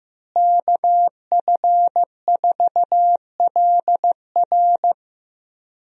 Morse code, heard as a keyed tone, about 15 wpm, 700 Hz. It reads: KF4LR
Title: USA Call Signs - 5 Characters - Repeated 3x - 15wpm